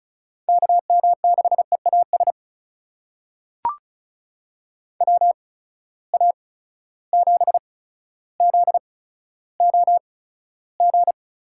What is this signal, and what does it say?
Morse code, heard as a keyed tone, about 36 wpm, spaced as for 15 wpm, 700 Hz. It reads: KM6EAS WA7ZOG
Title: USA Call Signs - 6 Characters - 8x Character Spacing - 35wpm